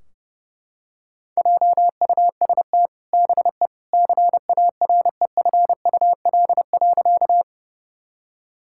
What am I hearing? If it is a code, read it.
Morse code, heard as a keyed tone, about 30 wpm, 700 Hz. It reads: JUST BE CAREFUL.
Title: Sentences from Top 1600 Words 30wpm